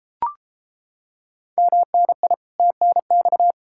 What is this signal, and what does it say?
Morse code, heard as a keyed tone, about 33 wpm, 700 Hz. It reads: MNI TNX